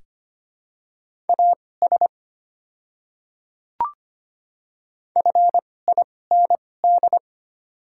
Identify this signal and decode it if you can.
Morse code, heard as a keyed tone, about 25 wpm, spaced as for 18 wpm, 700 Hz. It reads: AS FIND
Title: 2x Character Spacing - Top 100 Words - 25wpm